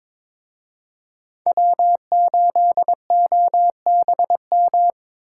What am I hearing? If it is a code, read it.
Morse code, heard as a keyed tone, about 26 wpm, 700 Hz. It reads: W8OBM
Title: Speed Racing - USA 1x3 Call Signs 22wpm